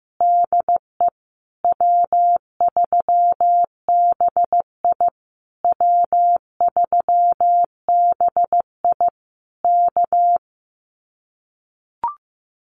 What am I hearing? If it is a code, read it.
Morse code, heard as a keyed tone, about 15 wpm, 700 Hz. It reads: DE W3BI W3BI K
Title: Common QSO Phrases Encore 15wpm